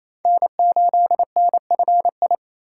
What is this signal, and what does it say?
Morse code, heard as a keyed tone, about 28 wpm, 700 Hz. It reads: N8NFI
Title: USA Call Signs - 5 Characters - 4x Character Spacing - 28wpm